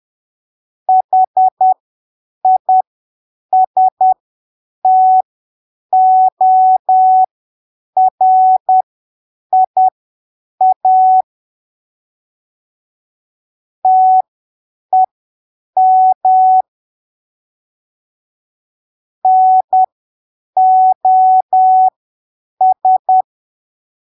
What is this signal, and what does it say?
Morse code, heard as a keyed tone, about 10 wpm, spaced as for 7 wpm, 750 Hz. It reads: HISTORIA TEM NOS
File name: arquivos / texto / texto 05 10ppm